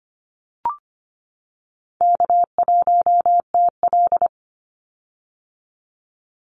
Morse code, heard as a keyed tone, about 25 wpm, 700 Hz. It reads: K1TL
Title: Call Signs - 25wpm